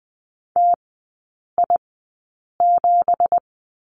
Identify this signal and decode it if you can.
Morse code, heard as a keyed tone, about 20 wpm, spaced as for 8 wpm, 700 Hz. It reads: TI7
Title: 166 Lesson 27e - Number 7 - Three Characters 20wpm